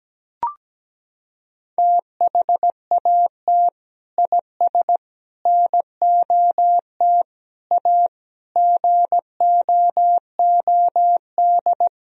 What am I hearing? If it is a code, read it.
Morse code, heard as a keyed tone, about 17 wpm, 700 Hz. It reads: THAT IS NOT A GOOD
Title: Sentences from Top 1000 Words - Review Part 3 - 17wpm